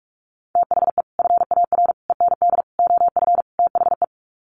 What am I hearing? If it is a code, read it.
Morse code, heard as a keyed tone, about 44 wpm, 700 Hz. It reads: THE FAR END OF THE
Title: Sets of 5 Words 45wpm